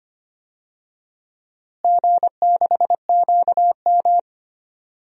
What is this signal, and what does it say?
Morse code, heard as a keyed tone, about 25 wpm, 700 Hz. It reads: G6QM
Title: DXCC-100 Call Signs - 4 Characters - Repeated 2x - 25wpm